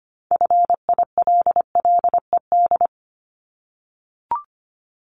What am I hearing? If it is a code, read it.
Morse code, heard as a keyed tone, about 25 wpm, 700 Hz. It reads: FILLED
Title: Multi-letter Phonemes Repeated 3x 25wpm